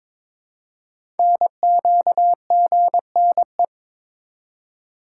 Morse code, heard as a keyed tone, about 23 wpm, 700 Hz. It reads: NQGNE
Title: Code Groups - 5 Letters - 22wpm